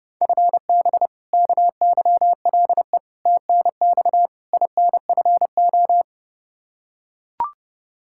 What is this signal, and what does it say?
Morse code, heard as a keyed tone, about 30 wpm, 700 Hz. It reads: FB KYLE TNX INFO
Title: Practice QSO - Exchange 3 - More Info - 30wpm